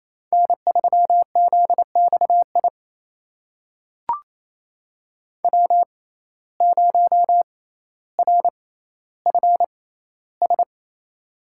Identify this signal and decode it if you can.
Morse code, heard as a keyed tone, about 28 wpm, spaced as for 14 wpm, 700 Hz. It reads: N3ZXI W0RFS
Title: USA Call Signs - 5 Characters - 6x Character Spacing - 28wpm